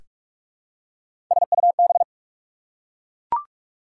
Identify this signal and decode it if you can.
Morse code, heard as a keyed tone, about 44 wpm, 690 Hz. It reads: SUB
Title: Prefixes and Suffixes 45wpm